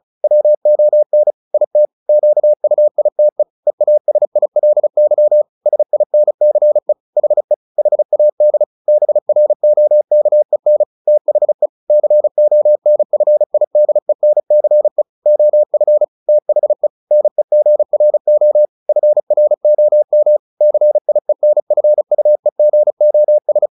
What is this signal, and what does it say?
Morse code, heard as a keyed tone, about 34 wpm, 600 Hz. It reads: WON IT QUITE EASILY SINCE HE HAD BROKEN THE CONFIDENCE OF THE NEGRO FROM CIENFUEGOS
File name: the old man and the sea 2 35wpm